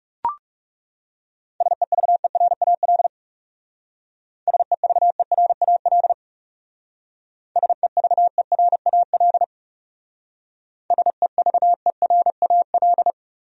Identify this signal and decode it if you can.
Morse code, heard as a keyed tone, about 35 wpm, 700 Hz. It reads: SEVERAL SEVERAL SEVERAL SEVERAL
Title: Speed-Racing Top 500 Words 30wpm